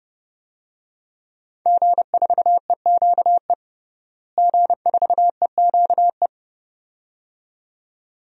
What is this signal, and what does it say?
Morse code, heard as a keyed tone, about 32 wpm, 700 Hz. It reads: G4EQE G4EQE
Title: DXCC-100 Call Signs - 5 Characters - Repeated 2x - 30wpm